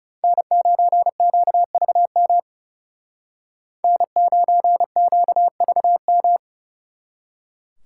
Morse code, heard as a keyed tone, about 38 wpm, 700 Hz. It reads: N9QVM N9QVM